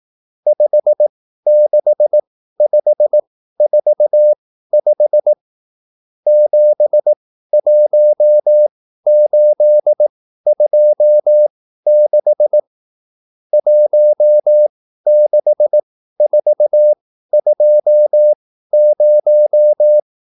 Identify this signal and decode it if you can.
Morse code, heard as a keyed tone, about 18 wpm, spaced as for 13 wpm, 600 Hz. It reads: 56545 71826 16420